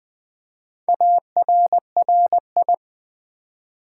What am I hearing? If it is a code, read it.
Morse code, heard as a keyed tone, about 20 wpm, 690 Hz. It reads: ARRI
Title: Build A Word - 2 Letters At A Time - 6-Letter Words - 20wpm